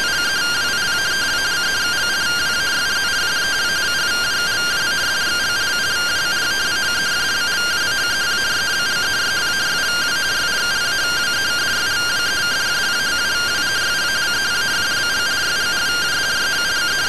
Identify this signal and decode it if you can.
Two-tone teleprinter signal, about 170 Hz shift.